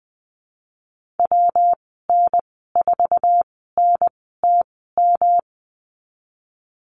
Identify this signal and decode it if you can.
Morse code, heard as a keyed tone, about 20 wpm, spaced as for 15 wpm, 700 Hz. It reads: WN4NTM
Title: 122 Lesson 20g - Number 3 - Call Signs 20wpm